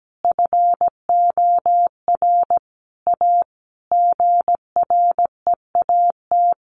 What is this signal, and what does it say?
Morse code, heard as a keyed tone, about 17 wpm, 700 Hz. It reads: FOR A GREAT
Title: Sentences from Top 1200 Words 17wpm